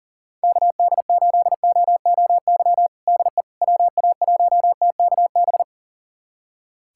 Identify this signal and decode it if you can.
Morse code, heard as a keyed tone, about 40 wpm, 700 Hz. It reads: KD8OOY DE WA1TKB
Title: USA Callsign DE Callsign - 6 Characters - 40wpm